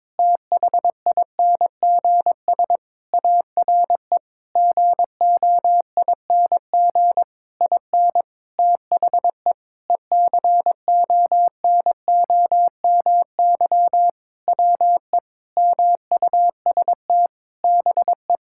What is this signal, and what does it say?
Morse code, heard as a keyed tone, about 22 wpm, 700 Hz. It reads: THINGS ARE GOING IN THE ECONOMY WE MUST BE
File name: practice / 22wpm / Sentences with 8-Word Phrases - 22wpm